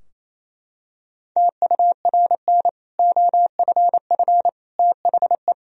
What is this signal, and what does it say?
Morse code, heard as a keyed tone, about 28 wpm, 700 Hz. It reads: TURN OFF THE
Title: Sentences from Top 1000 Words - Review Part 3 - 28wpm